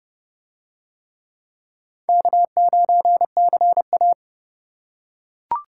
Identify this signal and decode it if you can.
Morse code, heard as a keyed tone, about 29 wpm, 700 Hz. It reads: K9CA